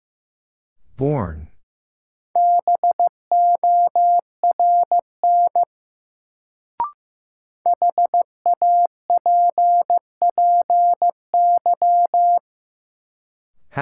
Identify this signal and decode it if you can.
Morse code, heard as a keyed tone, about 15 wpm, 700 Hz. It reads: BORN HAPPY
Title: Sentences from Top 1000 Words - New Word Drill - 15wpm